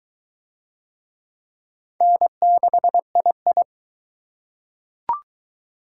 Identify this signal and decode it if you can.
Morse code, heard as a keyed tone, about 21 wpm, 700 Hz. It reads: N6II